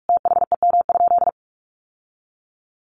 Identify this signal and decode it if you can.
Morse code, heard as a keyed tone, about 44 wpm, 700 Hz. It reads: THEM?
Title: Sentences from Top 1200 Words 45wpm